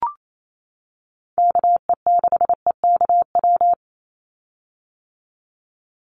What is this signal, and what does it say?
Morse code, heard as a keyed tone, about 28 wpm, 700 Hz. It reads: KE6EKW
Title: USA Call Signs 28wpm